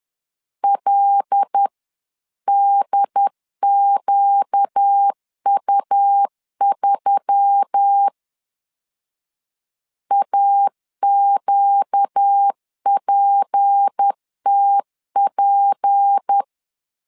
Morse code, heard as a keyed tone, about 10 wpm, 800 Hz. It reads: LDQU3 AQPTP
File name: LEZIONI / 6 PKX0 / 6 P 10wpm 2432